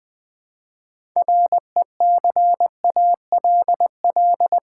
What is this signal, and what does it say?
Morse code, heard as a keyed tone, about 20 wpm, 700 Hz. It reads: RECALL
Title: Build A Word - 2 Letters At A Time - 6-Letter Words - 20wpm